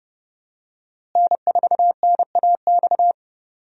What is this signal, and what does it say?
Morse code, heard as a keyed tone, about 30 wpm, 700 Hz. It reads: N4NAX
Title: USA Call Signs - 5 Characters - Repeated 3x - 30wpm